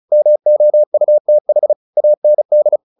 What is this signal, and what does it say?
Morse code, heard as a keyed tone, about 34 wpm, 600 Hz. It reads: MOUTH AND
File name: the old man and the sea 2 35wpm